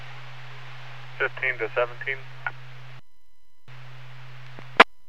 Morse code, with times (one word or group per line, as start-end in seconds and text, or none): none